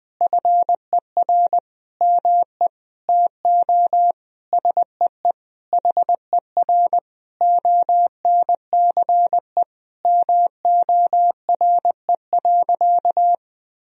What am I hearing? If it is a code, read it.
Morse code, heard as a keyed tone, about 21 wpm, 700 Hz. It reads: FER ME TO SEE HER ONCE MORE.
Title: Sentences from Top 500 Words and Abbreviations 20wpm